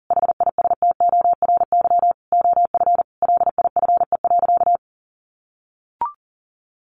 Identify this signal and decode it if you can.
Morse code, heard as a keyed tone, about 40 wpm, 700 Hz. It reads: HISTORY OF LIFE.
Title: A Sound Of Thunder 40wpm